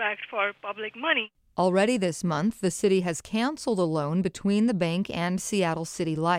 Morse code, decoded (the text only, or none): none